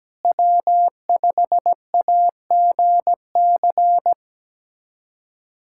Morse code, heard as a keyed tone, about 17 wpm, 700 Hz. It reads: W5AGC
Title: USA Callsign DE Callsign - 5 Characters - 17wpm